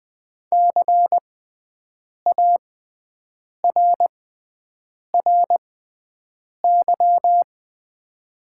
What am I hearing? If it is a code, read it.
Morse code, heard as a keyed tone, about 20 wpm, spaced as for 7 wpm, 700 Hz. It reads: CARRY
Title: 6x Character Spacing - Top 300 Words - 20wpm